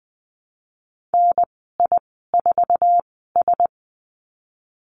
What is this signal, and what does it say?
Morse code, heard as a keyed tone, about 20 wpm, spaced as for 14 wpm, 700 Hz. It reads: NI4S
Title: 048 Lesson 9g - Number 4 - Call Signs 20wpm